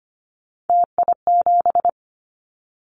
Morse code, heard as a keyed tone, about 25 wpm, 700 Hz. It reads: TI7
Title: Callsign Prefixes for Top 100 DXCC - 25wpm